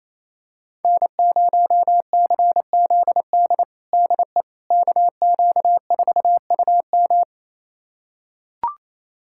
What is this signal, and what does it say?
Morse code, heard as a keyed tone, about 28 wpm, 700 Hz. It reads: N0CZD DE KQ4UM
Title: USA Callsign DE Callsign - 5 Characters - 28wpm